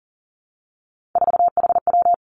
Morse code, heard as a keyed tone, about 41 wpm, 700 Hz. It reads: 4HW
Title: Three Challenging Letter-Number Combinations 40wpm